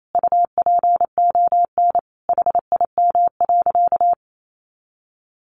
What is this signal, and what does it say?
Morse code, heard as a keyed tone, about 28 wpm, 700 Hz. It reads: UPON HIM.
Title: Sentences from Top 1100 Words 28wpm